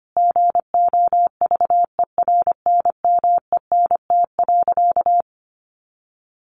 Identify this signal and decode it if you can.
Morse code, heard as a keyed tone, about 25 wpm, 700 Hz. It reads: GOVERNMENT.
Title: Easy Phrases 25wpm